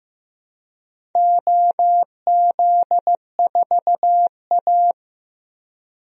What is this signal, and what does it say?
Morse code, heard as a keyed tone, about 15 wpm, 700 Hz. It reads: OZ4A
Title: DXCC-100 Call Signs - 4 Characters - 15wpm